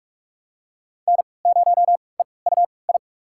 Morse code, heard as a keyed tone, about 46 wpm, spaced as for 26 wpm, 700 Hz. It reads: N0EUI
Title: USA Call Signs - 3x Character Spacing - 45wpm